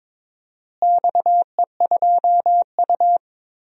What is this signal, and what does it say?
Morse code, heard as a keyed tone, about 22 wpm, 700 Hz. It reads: XE2U
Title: DXCC-100 Call Signs - 4 Characters - Repeated 3x - 22wpm